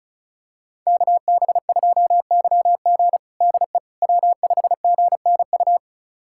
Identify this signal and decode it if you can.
Morse code, heard as a keyed tone, about 36 wpm, 700 Hz. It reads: KB2YG DE W5GNU
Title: USA Callsign DE Callsign - 5 Characters - 35wpm